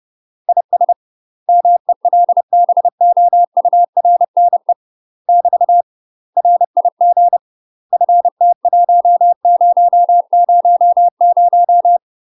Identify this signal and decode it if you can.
Morse code, heard as a keyed tone, about 30 wpm, 700 Hz. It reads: IS MELBOURNE <BT> RIG FT1000
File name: UK MASTER QSOs TRAINING / 30WPM / UK MASTER QSOs 0029